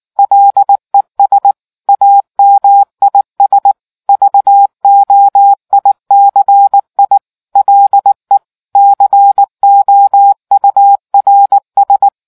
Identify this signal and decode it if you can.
Morse code, heard as a keyed tone, about 19 wpm, 800 Hz. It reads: LES AMIS VOICI LE COURS